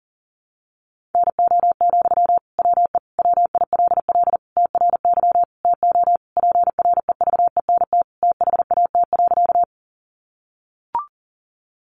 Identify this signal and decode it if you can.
Morse code, heard as a keyed tone, about 40 wpm, 700 Hz. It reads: NO, WE WILL TRY TO PREVENT THAT.
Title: Sentences from Top 1100 Words 40wpm